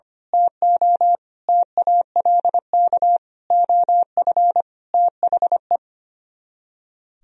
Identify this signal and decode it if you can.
Morse code, heard as a keyed tone, about 25 wpm, 700 Hz. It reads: TO TALK OF THE
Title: Speed-Racing - Sets of 5 Words 25wpm